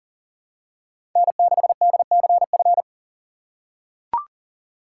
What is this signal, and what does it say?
Morse code, heard as a keyed tone, about 40 wpm, 700 Hz. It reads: N6DCF
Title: USA Call Signs - 5 Characters - 6x Character Spacing - 40wpm